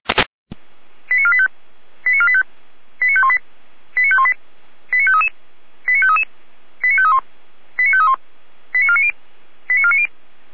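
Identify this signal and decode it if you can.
Five-tone selective call (ZVEI), heard as quick runs of five tones, tones of about 70 ms. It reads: 89485 89485 89518 89518 89533 89533 89531 89531 89490 89490
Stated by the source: ZVEI five-tone selective call received on SDR.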